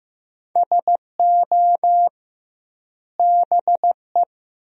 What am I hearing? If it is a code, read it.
Morse code, heard as a keyed tone, about 15 wpm, 700 Hz. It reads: SO BE
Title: 2x Word-Spacing - Sentences from Top 100 Words - 15wpm